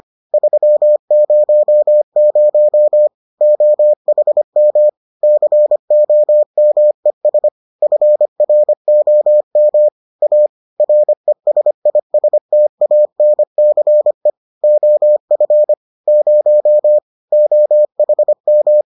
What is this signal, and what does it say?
Morse code, heard as a keyed tone, about 25 wpm, 600 Hz. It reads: 300 OHM COMES FROM A RESISTANCE OF 0 OHM